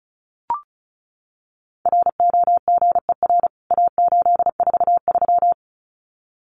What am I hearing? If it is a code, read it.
Morse code, heard as a keyed tone, about 34 wpm, 700 Hz. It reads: ROGER A843
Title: CWT Exchange 35wpm